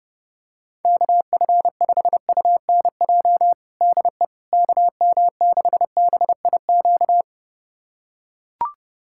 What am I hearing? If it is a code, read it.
Morse code, heard as a keyed tone, about 30 wpm, 700 Hz. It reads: KF5UNJ DE KM6BIQ
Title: USA Callsign DE Callsign 30wpm